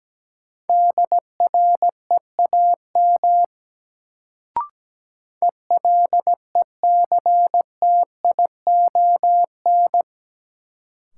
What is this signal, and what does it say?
Morse code, heard as a keyed tone, about 17 wpm, 700 Hz. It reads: DREAM ELECTION
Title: Sentences from Top 1500 Words - New Word Drill - 17wpm